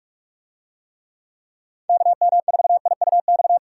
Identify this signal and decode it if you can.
Morse code, heard as a keyed tone, about 44 wpm, 700 Hz. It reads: KM4IUX